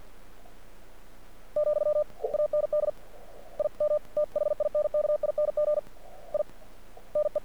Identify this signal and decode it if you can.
Morse code, heard as a keyed tone, about 48 wpm, 600 Hz. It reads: , AND I M THINKING I NE
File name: archived / CRSnet